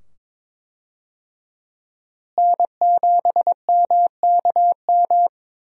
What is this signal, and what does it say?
Morse code, heard as a keyed tone, about 22 wpm, 700 Hz. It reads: N7MKM
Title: USA Call Signs - 5 Characters - Repeated 2x - 22wpm